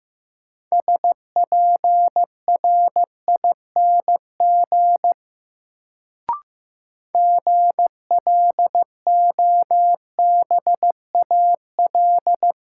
Morse code, heard as a keyed tone, about 15 wpm, 700 Hz. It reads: SPRING GLOBAL
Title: Sentences from Top 1200 Words - New Word Drill Alt - 15wpm